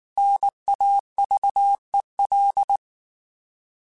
Morse code, heard as a keyed tone, about 19 wpm, 800 Hz. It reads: NAVEL
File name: Alphabetical / Alphabetical mp3s / Ns @18